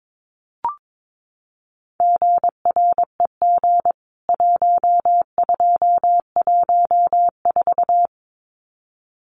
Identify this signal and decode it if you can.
Morse code, heard as a keyed tone, about 22 wpm, 700 Hz. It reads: GREG 1214
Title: CWT Exchange 22wpm